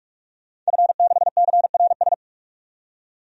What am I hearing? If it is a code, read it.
Morse code, heard as a keyed tone, about 42 wpm, 700 Hz. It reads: F6CRS
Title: DXCC-100 Call Signs - 5 Characters - 45wpm